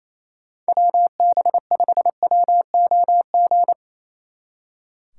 Morse code, heard as a keyed tone, about 34 wpm, 700 Hz. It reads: WB5WOG
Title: Speed Racing - USA 2x3 Call Signs 28wpm